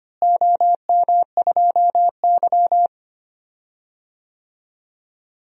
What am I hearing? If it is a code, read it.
Morse code, heard as a keyed tone, about 25 wpm, 700 Hz. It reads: OM2Y
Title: DXCC-100 Call Signs - 4 Characters - Repeated 2x - 25wpm